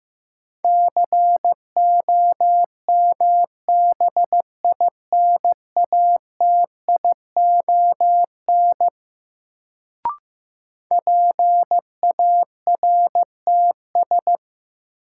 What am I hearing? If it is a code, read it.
Morse code, heard as a keyed tone, about 15 wpm, 700 Hz. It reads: COMBINATION PARTS
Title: Sentences from Top 1000 Words - Review Part 6 - New Word Drill - 15wpm